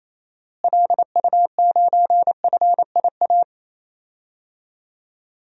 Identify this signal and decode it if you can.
Morse code, heard as a keyed tone, about 28 wpm, 700 Hz. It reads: LU9FIA
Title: DXCC-100 Call Signs - 6 Characters - Repeated 3x - 28wpm